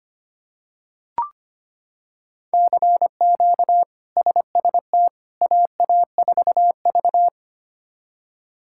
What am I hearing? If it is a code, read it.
Morse code, heard as a keyed tone, about 25 wpm, 700 Hz. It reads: CQ SST AA4V
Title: K1USN SST QSO Practice - 25wpm